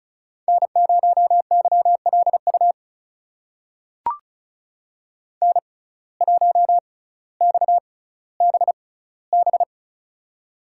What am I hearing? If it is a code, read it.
Morse code, heard as a keyed tone, about 36 wpm, spaced as for 18 wpm, 700 Hz. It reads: N0YLU N1XBB
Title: USA Call Signs - 5 Characters - 6x Character Spacing - 35wpm